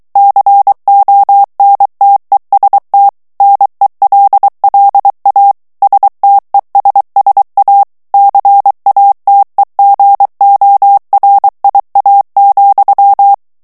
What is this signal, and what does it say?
Morse code, heard as a keyed tone, about 23 wpm, 800 Hz. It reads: CONTEST NELLA STESSA CATEGORIA,
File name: ITALIEN texte clair 2 vitesse 22 mots minute